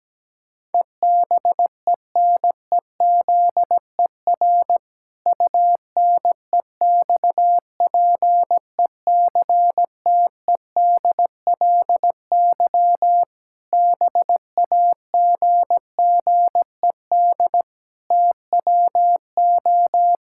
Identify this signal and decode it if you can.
Morse code, heard as a keyed tone, about 17 wpm, 700 Hz. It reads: EBENEZER UNEXPECTEDLY BAGGED TWO
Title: Pangram Challenge v2 - 17wpm